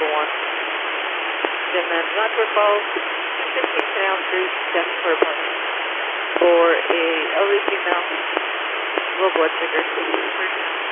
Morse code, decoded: none

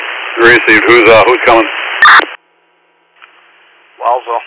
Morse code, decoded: none

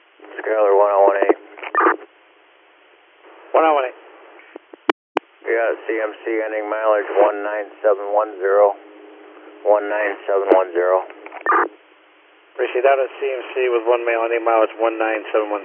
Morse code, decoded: none

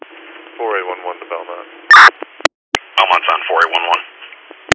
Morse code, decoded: none

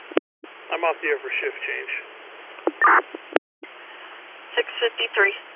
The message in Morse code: none

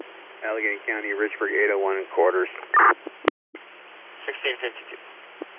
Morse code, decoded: none

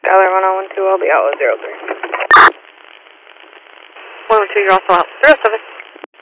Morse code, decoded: none

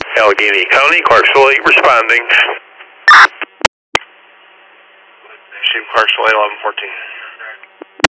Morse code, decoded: none

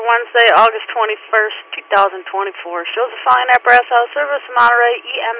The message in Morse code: none

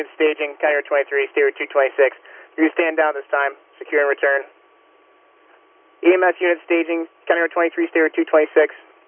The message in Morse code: none

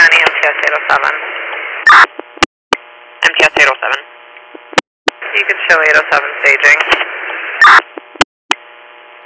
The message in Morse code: none

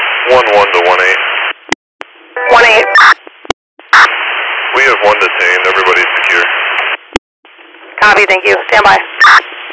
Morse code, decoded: none

